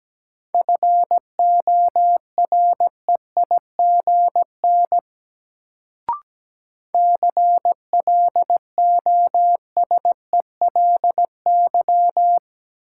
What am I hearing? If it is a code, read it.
Morse code, heard as a keyed tone, about 17 wpm, 700 Hz. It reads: FOREIGN CLOSELY
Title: Sentences from Top 1400 Words - New Word Drill - 17wpm